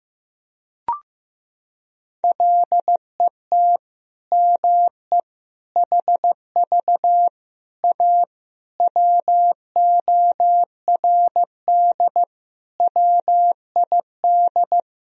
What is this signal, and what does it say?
Morse code, heard as a keyed tone, about 15 wpm, 700 Hz. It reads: LET ME HV A WORD WID
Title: Sentences from Top 300 Words and Abbreviations 15wpm